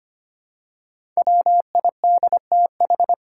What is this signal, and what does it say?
Morse code, heard as a keyed tone, about 25 wpm, 700 Hz. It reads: WIDTH
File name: 5 Letter Words - First 3 Letters Given - 25wpm